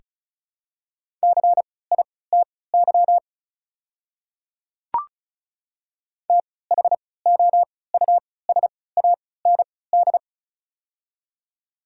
Morse code, heard as a keyed tone, about 36 wpm, spaced as for 20 wpm, 700 Hz. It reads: CITY THOUSAND
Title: 3x Character Spacing - Top 500 Words - 35wpm